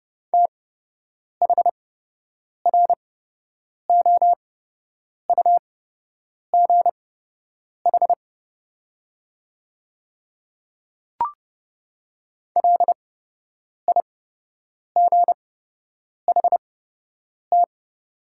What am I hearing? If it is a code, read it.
Morse code, heard as a keyed tone, about 31 wpm, spaced as for 8 wpm, 700 Hz. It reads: THROUGH LIGHT
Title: 8x Character Spacing - Top 200 Words - 30wpm